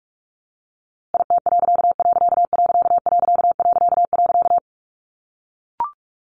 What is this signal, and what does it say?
Morse code, heard as a keyed tone, about 44 wpm, 700 Hz. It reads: IT......
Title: A Sound Of Thunder 45wpm